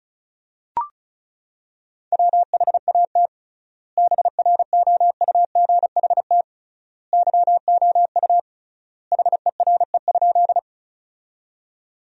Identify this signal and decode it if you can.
Morse code, heard as a keyed tone, about 34 wpm, 700 Hz. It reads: WHAT BROUGHT YOU HERE?
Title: 3x Word-Spacing - Sentences from Top 500 Words - 35wpm